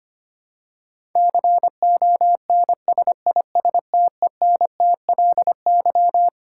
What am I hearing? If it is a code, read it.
Morse code, heard as a keyed tone, about 25 wpm, 700 Hz. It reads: CONSISTENTLY
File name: Twelve Letter Words - 25wpm